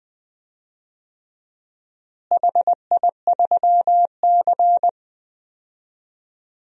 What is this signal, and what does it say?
Morse code, heard as a keyed tone, about 20 wpm, 700 Hz. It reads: HI3C
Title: DXCC-100 Call Signs - 4 Characters - Repeated 2x - 20wpm